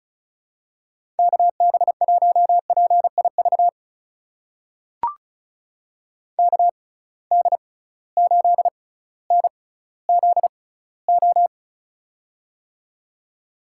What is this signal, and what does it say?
Morse code, heard as a keyed tone, about 36 wpm, spaced as for 18 wpm, 700 Hz. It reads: KB1PIV KD8NZO
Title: USA Call Signs - 6 Characters - 6x Character Spacing - 35wpm